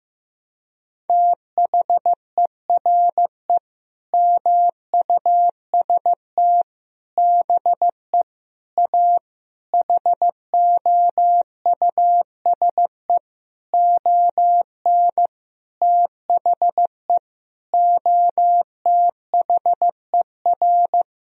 Morse code, heard as a keyed tone, about 15 wpm, 700 Hz. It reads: THERE MUST BE A HOUSE ON THE OTHER
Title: Sentences from Top 1000 Words - Review Part 1 - 15wpm